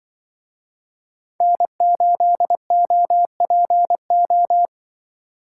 Morse code, heard as a keyed tone, about 24 wpm, 700 Hz. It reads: N8OPO
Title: Speed Racing - USA 1x3 Call Signs 20wpm